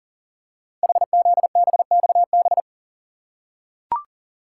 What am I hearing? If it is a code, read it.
Morse code, heard as a keyed tone, about 38 wpm, 700 Hz. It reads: HZBXB
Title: Code Groups - 5 Letters - 40wpm